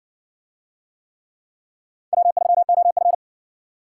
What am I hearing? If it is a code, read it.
Morse code, heard as a keyed tone, about 58 wpm, 700 Hz. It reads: W3OF